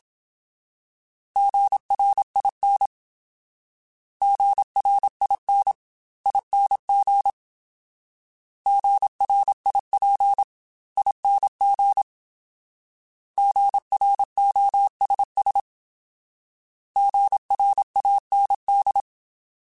Morse code, heard as a keyed tone, about 27 wpm, 800 Hz. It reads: GRIN GRIN ING GRIP ING GROSS GRAND
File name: Alphabetical / Alphabetical mp3s / Gs @25